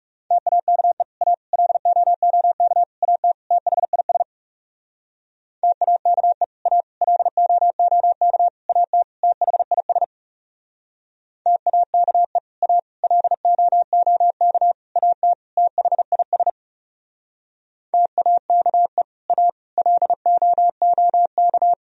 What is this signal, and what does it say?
Morse code, heard as a keyed tone, about 39 wpm, 700 Hz. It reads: TAKE A LOOK AT THIS TAKE A LOOK AT THIS TAKE A LOOK AT THIS TAKE A LOOK